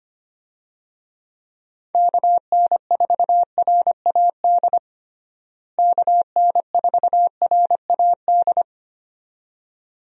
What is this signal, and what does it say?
Morse code, heard as a keyed tone, about 25 wpm, 700 Hz. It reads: KN4RAD KN4RAD
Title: USA Call Signs - 6 Characters - Repeated 2x - 25wpm